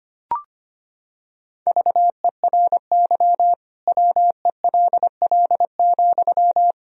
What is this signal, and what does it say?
Morse code, heard as a keyed tone, about 25 wpm, 700 Hz. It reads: VERY WELL,
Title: Sentences from Top 1000 Words - Review Part 4 - 25wpm